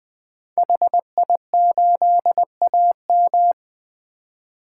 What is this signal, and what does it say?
Morse code, heard as a keyed tone, about 20 wpm, 700 Hz. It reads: HI8AM
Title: 2023 POTA Activator Call Signs - 20wpm